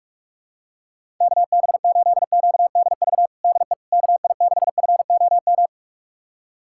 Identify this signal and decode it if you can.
Morse code, heard as a keyed tone, about 41 wpm, 700 Hz. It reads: KB8QDV DE KI6FOM
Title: USA Callsign DE Callsign - 6 Characters - 45wpm